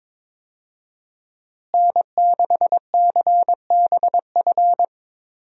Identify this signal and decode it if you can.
Morse code, heard as a keyed tone, about 22 wpm, 700 Hz. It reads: N6CBF
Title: USA Call Signs - 5 Characters - Repeated 2x - 22wpm